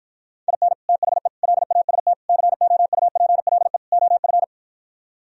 Morse code, heard as a keyed tone, about 58 wpm, 700 Hz. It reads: IN THE LAST COUPLE OF